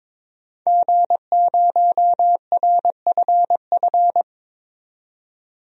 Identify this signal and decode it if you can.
Morse code, heard as a keyed tone, about 23 wpm, 700 Hz. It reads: G0RFF
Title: DXCC-100 Call Signs - 5 Characters - 22wpm